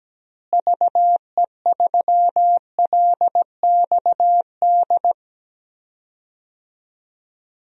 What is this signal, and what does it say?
Morse code, heard as a keyed tone, about 17 wpm, 700 Hz. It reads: VE3LXD
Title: DXCC-100 Call Signs - 6 Characters - Repeated 3x - 17wpm